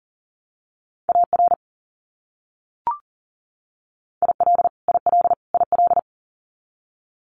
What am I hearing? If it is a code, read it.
Morse code, heard as a keyed tone, about 40 wpm, 700 Hz. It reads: AR IL IL IL
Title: US States Repeated 3x 40wpm